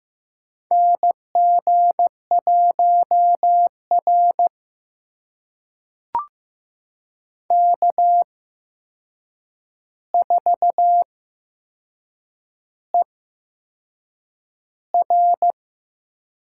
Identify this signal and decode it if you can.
Morse code, heard as a keyed tone, about 15 wpm, spaced as for 6 wpm, 700 Hz. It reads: NG1R K4ER
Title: USA Call Signs - 8x Character Spacing - 15wpm